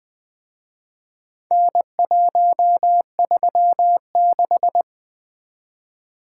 Morse code, heard as a keyed tone, about 24 wpm, 700 Hz. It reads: N136
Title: Speed-Racing - Four Letter-Number Combinations - 20wpm